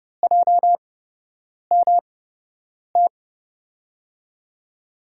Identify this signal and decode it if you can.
Morse code, heard as a keyed tone, about 32 wpm, spaced as for 8 wpm, 700 Hz. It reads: JMT
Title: USA Call Signs - 6 Characters - 8x Character Spacing - 30wpm